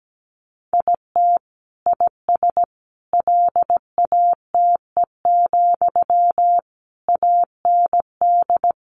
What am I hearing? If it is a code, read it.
Morse code, heard as a keyed tone, about 17 wpm, 700 Hz. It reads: IT IS LATE, AND
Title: Sentences from Top 600 Words 17wpm